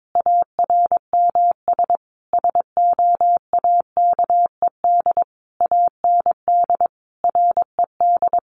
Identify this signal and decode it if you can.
Morse code, heard as a keyed tone, about 22 wpm, 700 Hz. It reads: ARMS SOAKED AND RED
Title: A Sound Of Thunder 22wpm